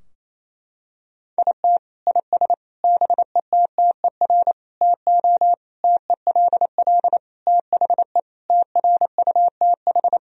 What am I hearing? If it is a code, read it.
Morse code, heard as a keyed tone, about 28 wpm, 700 Hz. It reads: IT IS BETTER TO TELL THE TRUTH
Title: Sentences from Top 1000 Words - Review Part 4 - 28wpm